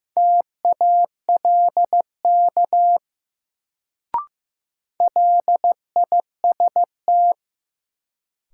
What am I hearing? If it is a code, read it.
Morse code, heard as a keyed tone, about 15 wpm, 700 Hz. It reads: TALK LIST
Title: Sentences from Top 400 Words - New Word Drill - 15wpm